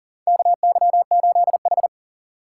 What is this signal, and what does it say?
Morse code, heard as a keyed tone, about 37 wpm, 700 Hz. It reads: KY8H
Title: USA Call Signs - 4 Characters - 2x Character Spacing - 40wpm